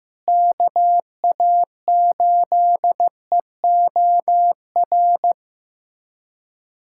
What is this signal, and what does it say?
Morse code, heard as a keyed tone, about 15 wpm, 700 Hz. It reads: KA8EOR
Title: USA Callsign DE Callsign - 6 Characters - 15wpm